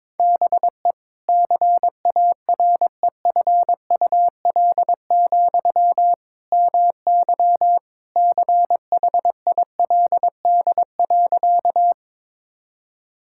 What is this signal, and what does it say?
Morse code, heard as a keyed tone, about 22 wpm, 700 Hz. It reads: BE CAREFUL, MY CHILD.
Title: Sentences from Top 1600 Words 22wpm